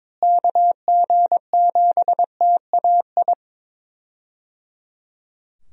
Morse code, heard as a keyed tone, about 22 wpm, 700 Hz. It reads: KG7TAI
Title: USA Call Signs - 6 Characters - Repeated 3x - 22wpm